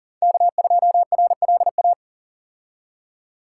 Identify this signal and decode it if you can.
Morse code, heard as a keyed tone, about 38 wpm, 690 Hz. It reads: K2RLA
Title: Code Groups - 5 Letter:Numbers - 40wpm